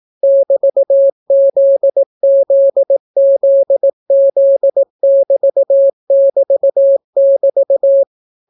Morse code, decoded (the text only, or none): <BT>ZZZZ<BT><BT><BT>